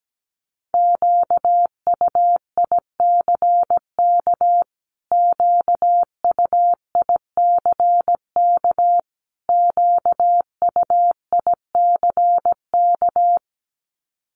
Morse code, heard as a keyed tone, about 17 wpm, 700 Hz. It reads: QUICK QUICK QUICK
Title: Top 500 Words Repeated 17wpm